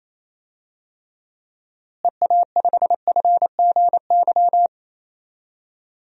Morse code, heard as a keyed tone, about 28 wpm, 700 Hz. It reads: EA5FGY
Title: DXCC-100 Call Signs - 6 Characters - Repeated 2x - 28wpm